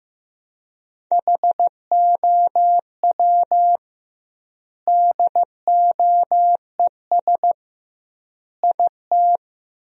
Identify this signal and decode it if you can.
Morse code, heard as a keyed tone, about 15 wpm, 700 Hz. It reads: HOW DOES IT
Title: 2x Word-Spacing - Sentences from Top 400 Words - 15wpm